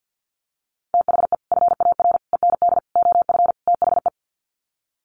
Morse code, heard as a keyed tone, about 48 wpm, 700 Hz. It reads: THE FAR END OF THE
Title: Sets of 5 Words 50wpm